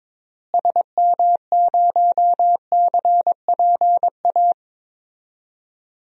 Morse code, heard as a keyed tone, about 22 wpm, 700 Hz. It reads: SM0CPA
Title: DXCC-100 Call Signs - 6 Characters - Repeated 3x - 22wpm